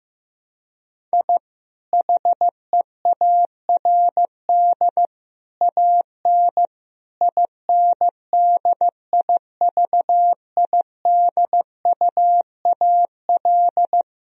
Morse code, heard as a keyed tone, about 15 wpm, 700 Hz. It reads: I HEARD AN INDIVIDUAL
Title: Sentences from Top 1000 Words - Review Part 4 - 15wpm